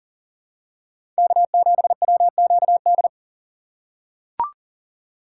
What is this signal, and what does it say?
Morse code, heard as a keyed tone, about 40 wpm, 700 Hz. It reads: K7WQD
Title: USA Call Signs - 5 Characters - 4x Character Spacing - 40wpm